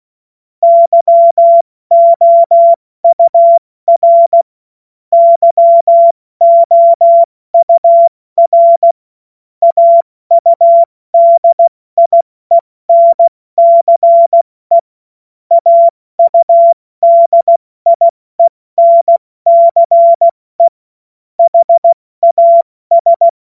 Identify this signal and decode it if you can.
Morse code, heard as a keyed tone, about 16 wpm, 680 Hz. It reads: YOUR YOUR AUDIENCE AUDIENCE HAS
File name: Morsekurs / Witze / Witz 27. September 2022 70 F80